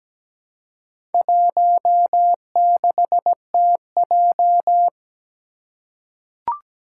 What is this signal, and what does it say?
Morse code, heard as a keyed tone, about 17 wpm, 700 Hz. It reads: 16TJ
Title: Four Challenging Letter-Number Combinations 17wpm